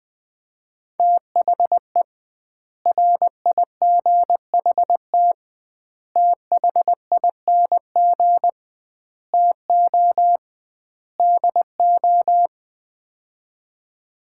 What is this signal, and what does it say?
Morse code, heard as a keyed tone, about 21 wpm, 700 Hz. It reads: THE RIGHT THING TO DO
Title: 2x Word-Spacing - Sets of 5 Words - 20wpm